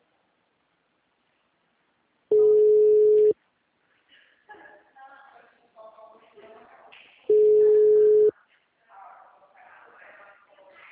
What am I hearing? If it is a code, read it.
Ringback tone (Europe, 425 Hz).